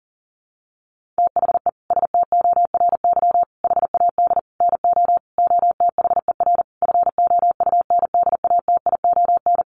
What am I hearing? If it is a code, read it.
Morse code, heard as a keyed tone, about 40 wpm, 700 Hz. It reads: THE STORY HAD NO OTHER FOUNDATION
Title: Sentences from Top 1200 Words 40wpm